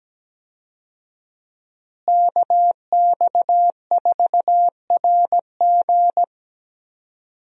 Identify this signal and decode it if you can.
Morse code, heard as a keyed tone, about 17 wpm, 700 Hz. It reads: KX4RG
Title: USA Call Signs - 5 Characters - Repeated 2x - 17wpm